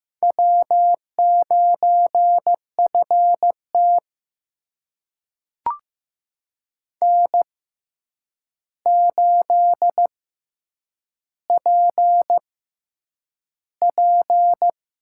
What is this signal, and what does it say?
Morse code, heard as a keyed tone, about 15 wpm, spaced as for 8 wpm, 700 Hz. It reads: W9FT N8PP
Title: USA Call Signs - 4 Characters - 6x Character Spacing - 15wpm